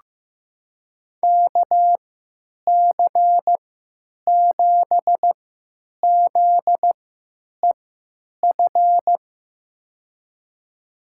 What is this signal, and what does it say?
Morse code, heard as a keyed tone, about 15 wpm, spaced as for 9 wpm, 700 Hz. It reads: KC7ZEF
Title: USA Call Signs - 6 Characters - 3x Character Spacing - 15wpm